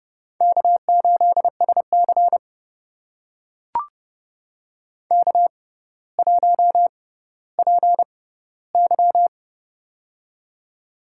Morse code, heard as a keyed tone, about 30 wpm, spaced as for 15 wpm, 700 Hz. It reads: K8SC K1PY
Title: USA Call Signs - 4 Characters - 6x Character Spacing - 30wpm